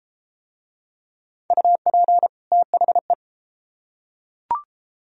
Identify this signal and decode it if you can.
Morse code, heard as a keyed tone, about 31 wpm, 700 Hz. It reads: UP THE